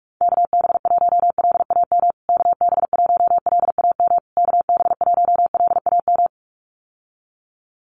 Morse code, heard as a keyed tone, about 44 wpm, 690 Hz. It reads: KB1LAM KB1LAM KB1LAM
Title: USA Call Signs Repeated 3x 45wpm